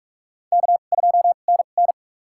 Morse code, heard as a keyed tone, about 44 wpm, spaced as for 32 wpm, 700 Hz. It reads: K2NN
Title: USA Call Signs - 4 Characters - 2x Character Spacing - 45wpm